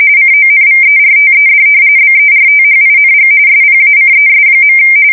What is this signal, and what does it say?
RTTY, about 45.45 baud. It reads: CONVERTED BERTHING SPACES.
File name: fsk170 / battleshipnj